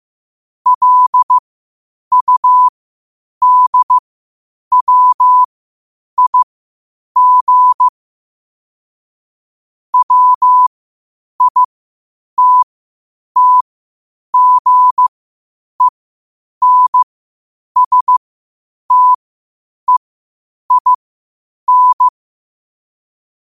Morse code, heard as a keyed tone, about 15 wpm, spaced as for 8 wpm, 1000 Hz. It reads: LUDWIG WITTGENSTEIN
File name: qotdfiles / 8wpm